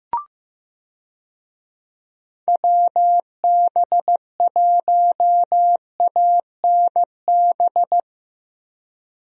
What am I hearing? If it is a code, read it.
Morse code, heard as a keyed tone, about 15 wpm, 700 Hz. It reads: WB1ANB
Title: USA Call Signs - 6 Characters - Repeated 3x - 15wpm